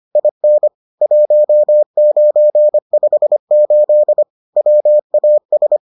Morse code, heard as a keyed tone, about 25 wpm, 600 Hz. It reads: IN 1958 WAS